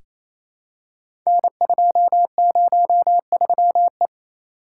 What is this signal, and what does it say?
Morse code, heard as a keyed tone, about 27 wpm, 700 Hz. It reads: N203E
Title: Code Groups - 5 Letter:Numbers - 28wpm